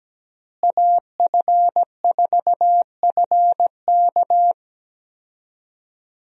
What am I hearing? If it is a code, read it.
Morse code, heard as a keyed tone, about 17 wpm, 700 Hz. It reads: AF4FK
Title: USA Call Signs - 5 Characters - Repeated 3x - 17wpm